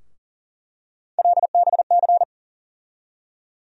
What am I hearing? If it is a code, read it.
Morse code, heard as a keyed tone, about 40 wpm, 700 Hz. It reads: LBC
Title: USA Call Sign Postfixes - 3 Letters 40wpm